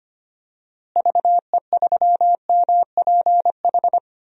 Morse code, heard as a keyed tone, about 25 wpm, 700 Hz. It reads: VE3MPH
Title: DXCC-100 Call Signs - 6 Characters - Repeated 3x - 25wpm